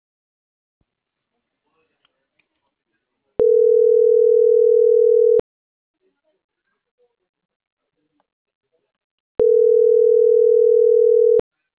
Ringback tone (North America), 440 + 480 Hz.